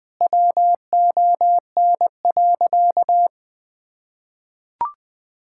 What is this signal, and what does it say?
Morse code, heard as a keyed tone, about 20 wpm, 700 Hz. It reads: WON.
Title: Sentences with 7-Word Phrases - Spoken Beginning - 20wpm